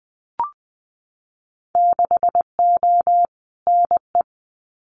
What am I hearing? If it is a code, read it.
Morse code, heard as a keyed tone, about 20 wpm, 700 Hz. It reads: 6O NE
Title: Winter Field Day Exchanges Encore 20wpm